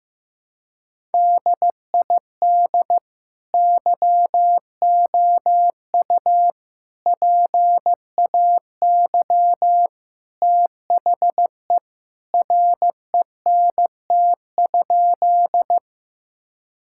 Morse code, heard as a keyed tone, about 15 wpm, 700 Hz. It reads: DID YOU PAY THE RENT?
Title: Sentences from Top 1000 Words - Review Part 5 - 15wpm